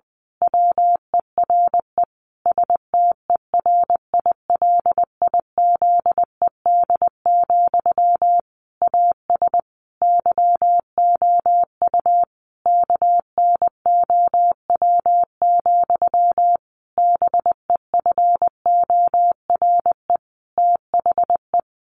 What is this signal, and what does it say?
Morse code, heard as a keyed tone, about 19 wpm, 700 Hz. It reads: WERE STERILIZED, AS YOU KNOW, BEFORE THE
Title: A Sound Of Thunder 20wpm